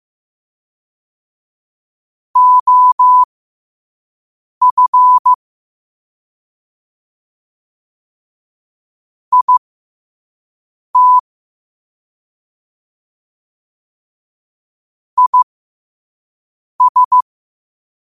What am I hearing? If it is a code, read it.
Morse code, heard as a keyed tone, about 15 wpm, spaced as for 5 wpm, 1000 Hz. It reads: OF IT IS
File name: qotdfiles / 5wpm